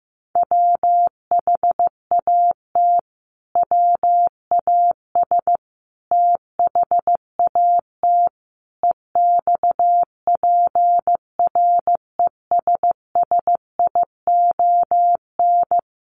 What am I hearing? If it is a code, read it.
Morse code, heard as a keyed tone, about 14 wpm, 700 Hz. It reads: WHAT WAS THAT EXPRESSION
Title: Sentences from Top 1200 Words 15wpm